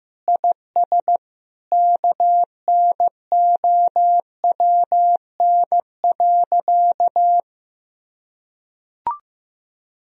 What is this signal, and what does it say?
Morse code, heard as a keyed tone, about 15 wpm, 700 Hz. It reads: IS KNOWN.
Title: Sentences from Top 1000 Words - Review Part 4 - 15wpm